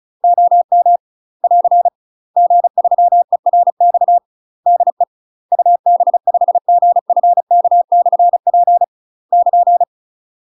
Morse code, heard as a keyed tone, about 35 wpm, 700 Hz. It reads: OM <AR> G3ERX DE UB5GFK/P <KN>
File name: UK MASTER QSOs TRAINING / 35WPM / UK MASTER QSOs 0026